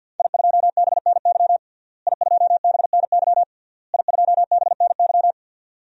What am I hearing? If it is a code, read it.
Morse code, heard as a keyed tone, about 52 wpm, 700 Hz. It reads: I2BNY I2BNY I2BNY
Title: DXCC-100 Call Signs - 5 Characters - Repeated 3x - 50wpm